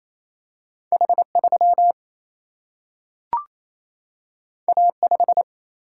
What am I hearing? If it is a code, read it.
Morse code, heard as a keyed tone, about 28 wpm, 700 Hz. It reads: H 3 A 5
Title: Single-Letter Single-Number 28wpm